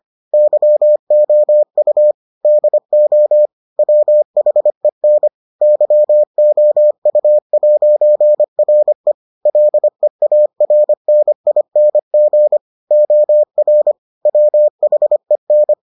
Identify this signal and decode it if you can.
Morse code, heard as a keyed tone, about 25 wpm, 600 Hz. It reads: YOU DO WHEN YOU'RE LEARNING OR WHEN